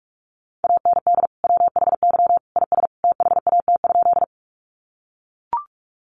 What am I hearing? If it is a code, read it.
Morse code, heard as a keyed tone, about 44 wpm, 700 Hz. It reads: AND WHY IS THAT?
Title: Sentences from Top 200 Words 45wpm